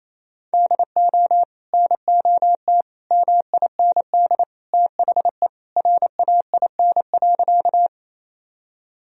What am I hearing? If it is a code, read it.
Morse code, heard as a keyed tone, about 28 wpm, 700 Hz. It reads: DO NOT MIND THE RAIN.
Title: Sentences from Top 1000 Words - Review Part 1 - 28wpm